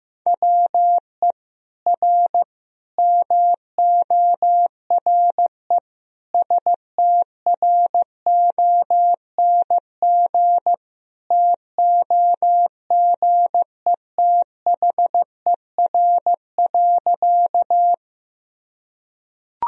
Morse code, heard as a keyed tone, about 15 wpm, 700 Hz. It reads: WE R MORE STRONG TOGETHER.
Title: Sentences from Top 500 Words and Abbreviations 15wpm